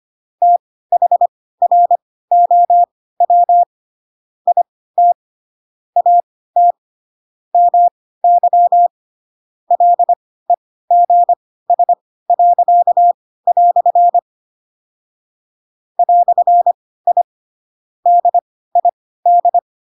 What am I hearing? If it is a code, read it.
Morse code, heard as a keyed tone, about 25 wpm, spaced as for 16 wpm, 700 Hz. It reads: THROW IT AT MY LEGS." "I DID
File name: Pinocchio / Pinocchio ch2 10 25 ik1hge